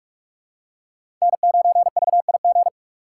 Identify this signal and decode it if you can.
Morse code, heard as a keyed tone, about 44 wpm, 690 Hz. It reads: N9VIG
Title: USA Call Signs - 5 Characters - 6x Character Spacing - 45wpm